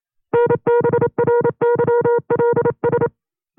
Morse code, heard as a keyed tone, about 28 wpm, 480 Hz. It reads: NBRYLS
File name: FreeFiles / CW-practice / 25WPM-side1